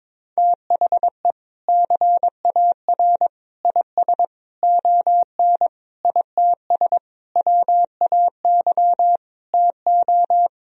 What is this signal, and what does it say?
Morse code, heard as a keyed tone, about 22 wpm, 700 Hz. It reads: THE CAR IS ON ITS WAY TO
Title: Sentences from Top 1000 Words - Review Part 1 - 22wpm